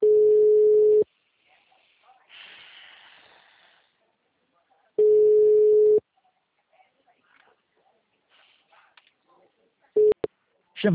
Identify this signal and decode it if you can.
Ringback tone (Europe, 425 Hz), with speech.